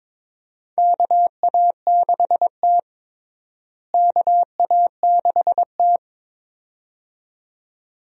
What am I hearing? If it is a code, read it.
Morse code, heard as a keyed tone, about 22 wpm, 700 Hz. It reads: KA6T KA6T
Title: USA Call Signs - 4 Characters - Repeated 2x - 22wpm